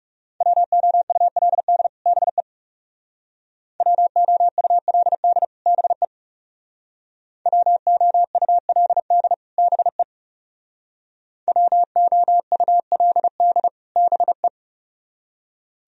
Morse code, heard as a keyed tone, about 38 wpm, 700 Hz. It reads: WOULD BE WOULD BE WOULD BE WOULD BE